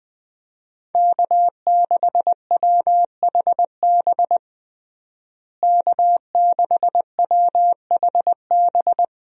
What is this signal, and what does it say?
Morse code, heard as a keyed tone, about 20 wpm, 700 Hz. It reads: K6WHB K6WHB
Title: USA Call Signs - 5 Characters - Repeated 2x - 20wpm